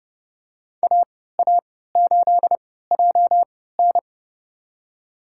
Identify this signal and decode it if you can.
Morse code, heard as a keyed tone, about 30 wpm, spaced as for 17 wpm, 700 Hz. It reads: AA8JN
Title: USA Call Signs - 5 Characters - 3x Character Spacing - 30wpm